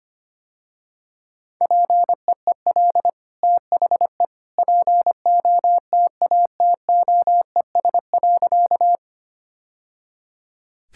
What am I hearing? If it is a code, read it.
Morse code, heard as a keyed tone, about 25 wpm, 700 Hz. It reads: PEEL THE POTATOES.
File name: Easy Sentences with Hints - 25wpm